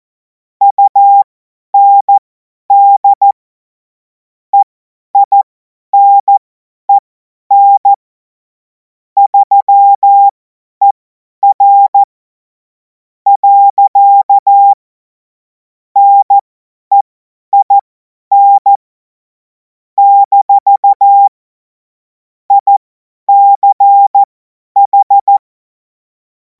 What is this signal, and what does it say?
Morse code, heard as a keyed tone, about 14 wpm, spaced as for 10 wpm, 800 Hz. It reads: UND EINEN 3ER. NEIN - ICH
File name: Morsekurs / Witze / Witz 31. März 2026 50 F70